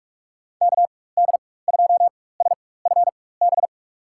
Morse code, heard as a keyed tone, about 42 wpm, spaced as for 21 wpm, 700 Hz. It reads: KD2SFB
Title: USA Call Signs - 6 Characters - 4x Character Spacing - 45wpm